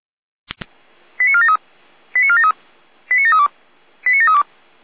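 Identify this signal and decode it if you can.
Five-tone selective call (ZVEI), heard as quick runs of five tones, tones of about 70 ms. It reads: 89473 89473 89732 89732
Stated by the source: ZVEI five-tone call 89345 received by SDR.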